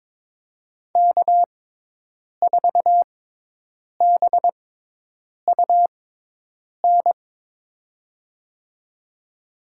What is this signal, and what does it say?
Morse code, heard as a keyed tone, about 22 wpm, spaced as for 8 wpm, 700 Hz. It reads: K4BUN
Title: USA Call Signs - 6x Character Spacing - 22wpm